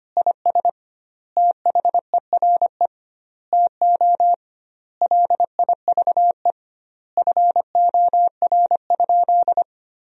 Morse code, heard as a keyed tone, about 25 wpm, 700 Hz. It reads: IS THERE TO LIVE FOR?
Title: 2x Word-Spacing - Sentences from Top 200 Words - 25wpm